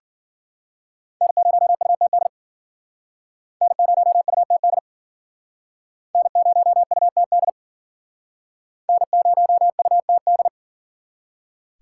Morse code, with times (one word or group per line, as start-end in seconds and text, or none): 1.2-2.3 N0UTD
3.6-4.8 N0UTD
6.1-7.5 N0UTD
8.9-10.5 N0UTD